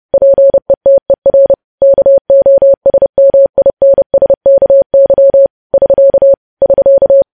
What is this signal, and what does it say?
Morse code, heard as a keyed tone, about 30 wpm, 550 Hz. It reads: PETER KOSMINSKY <SK> <SK>
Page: news headlines in morse code at 30 wpm